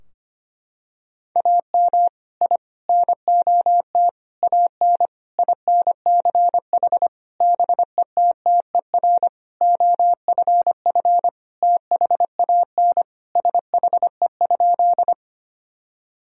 Morse code, heard as a keyed tone, about 25 wpm, 700 Hz. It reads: AM I NOT AN INCH BETTER OFF THAN SHE?
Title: Sentences from Top 1000 Words - Review Part 3 - 25wpm